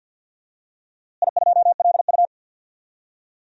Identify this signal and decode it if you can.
Morse code, heard as a keyed tone, about 47 wpm, 700 Hz. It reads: I2LU
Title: DXCC-100 Call Signs - 4 Characters - Repeated 2x - 50wpm